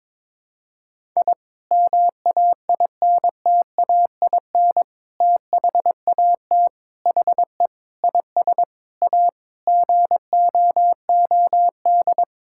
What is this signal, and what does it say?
Morse code, heard as a keyed tone, about 22 wpm, 700 Hz. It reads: I MAINTAIN THAT HE IS A GOOD
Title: Sentences from Top 1000 Words - Review Part 5 - 22wpm